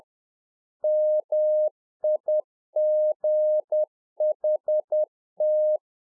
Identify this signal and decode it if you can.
Morse code, heard as a keyed tone, about 10 wpm, 610 Hz. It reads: MIGHT